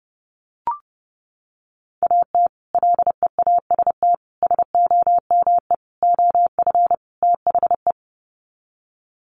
Morse code, heard as a keyed tone, about 30 wpm, 700 Hz. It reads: AT LEAST SOME OF THE
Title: Sets of 5 Words Encore 30wpm